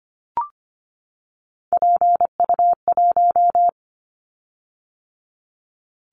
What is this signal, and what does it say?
Morse code, heard as a keyed tone, about 25 wpm, 700 Hz. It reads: PU1
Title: Callsign Prefixes for Top 50 DXCC - 25wpm